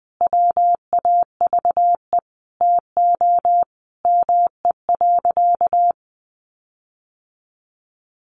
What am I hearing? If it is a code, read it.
Morse code, heard as a keyed tone, about 20 wpm, 700 Hz. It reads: WAVE TO ME.
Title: Sentences from Top 1400 Words 20wpm